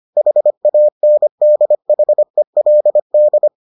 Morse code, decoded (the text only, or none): HANDHELD